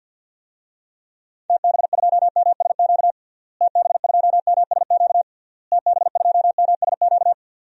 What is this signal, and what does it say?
Morse code, heard as a keyed tone, about 52 wpm, 700 Hz. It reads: TB2MSQ TB2MSQ TB2MSQ
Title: DXCC-100 Call Signs - 6 Characters - Repeated 3x - 50wpm